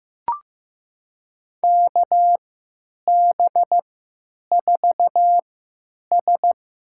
Morse code, heard as a keyed tone, about 15 wpm, spaced as for 8 wpm, 700 Hz. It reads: KB4S
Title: USA Call Signs - 3x Character Spacing - 15wpm